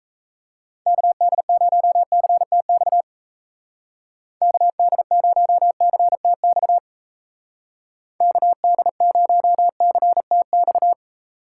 Morse code, heard as a keyed tone, about 34 wpm, 700 Hz. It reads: KD0CTX KD0CTX KD0CTX
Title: Speed Racing - USA 2x3 Call Signs 28wpm